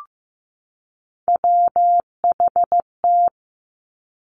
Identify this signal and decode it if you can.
Morse code, heard as a keyed tone, about 15 wpm, 700 Hz. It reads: WHT
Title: Three Challenging Letter-Number Combinations 15wpm